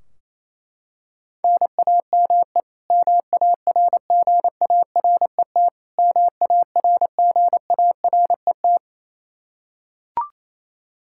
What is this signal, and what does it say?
Morse code, heard as a keyed tone, about 28 wpm, 700 Hz. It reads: NAME MARGARET MARGARET
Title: K1USN SST QSO Practice 28wpm